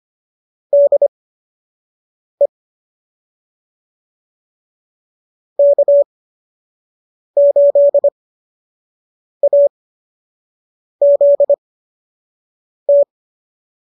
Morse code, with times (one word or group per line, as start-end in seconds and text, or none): 0.7-2.5 DE
5.6-13.0 K8AZT